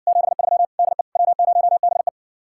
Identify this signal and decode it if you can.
Morse code, heard as a keyed tone, about 60 wpm, 700 Hz. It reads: 73 DE W0BE